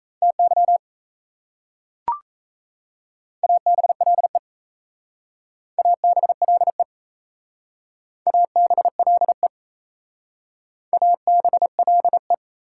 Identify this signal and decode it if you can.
Morse code, heard as a keyed tone, about 38 wpm, 700 Hz. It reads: TY ABLE ABLE ABLE ABLE